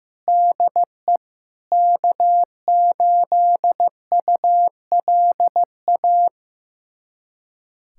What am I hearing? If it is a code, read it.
Morse code, heard as a keyed tone, about 16 wpm, 700 Hz. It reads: DE K8ULA
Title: USA Callsign DE Callsign 15wpm